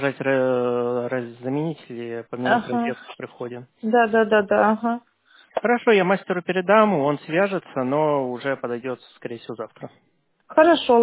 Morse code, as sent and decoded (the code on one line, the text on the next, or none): none